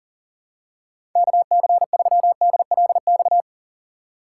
Morse code, heard as a keyed tone, about 40 wpm, 700 Hz. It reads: KC3DLX
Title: USA Call Signs - 6 Characters - 2x Character Spacing - 40wpm